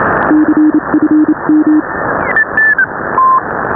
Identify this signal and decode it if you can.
Morse code, heard as a keyed tone, about 27 wpm, 310 Hz. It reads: CFM